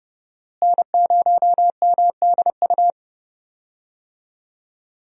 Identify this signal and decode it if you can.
Morse code, heard as a keyed tone, about 30 wpm, 700 Hz. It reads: N0MDU
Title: USA Call Signs - 5 Characters - Repeated 3x - 30wpm